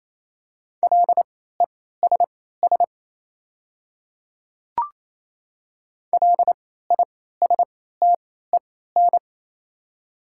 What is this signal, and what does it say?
Morse code, heard as a keyed tone, about 28 wpm, spaced as for 16 wpm, 700 Hz. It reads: LESS LISTEN
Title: 3x Character Spacing - Top 500 Words - 28wpm